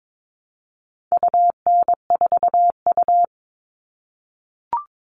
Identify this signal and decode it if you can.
Morse code, heard as a keyed tone, about 22 wpm, 700 Hz. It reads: UN4U
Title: All Cut Numbers - 4-Digit 22wpm